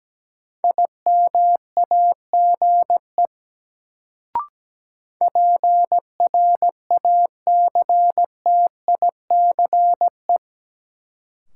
Morse code, heard as a keyed tone, about 17 wpm, 700 Hz. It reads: IMAGE PRACTICE
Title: Sentences from Top 700 Words - New Word Drill - 17wpm